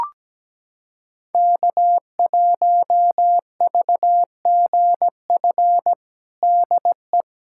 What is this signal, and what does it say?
Morse code, heard as a keyed tone, about 17 wpm, 700 Hz. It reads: K1VGF DE
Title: USA Callsign DE Callsign - 5 Characters - 17wpm